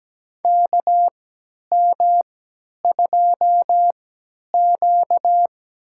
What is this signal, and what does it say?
Morse code, heard as a keyed tone, about 17 wpm, spaced as for 10 wpm, 700 Hz. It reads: KM2Q
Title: USA Call Signs - 4 Characters - 3x Character Spacing - 17wpm